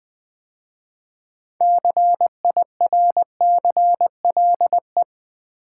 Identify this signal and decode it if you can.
Morse code, heard as a keyed tone, about 20 wpm, 700 Hz. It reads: CIRCLE
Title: 6 Letter Words - First 3 Letters Given - 20wpm